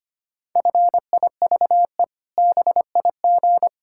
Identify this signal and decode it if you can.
Morse code, heard as a keyed tone, about 25 wpm, 700 Hz. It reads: FIVE BIG
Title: Pangram Challenge v2 - 25wpm